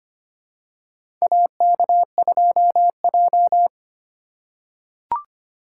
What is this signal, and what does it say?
Morse code, heard as a keyed tone, about 25 wpm, 700 Hz. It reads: AK2J
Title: USA Call Signs - 2x Character Spacing - 25wpm